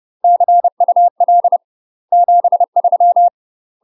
Morse code, heard as a keyed tone, about 30 wpm, 700 Hz. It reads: CUL 73
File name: UK MASTER QSOs TRAINING/30WPM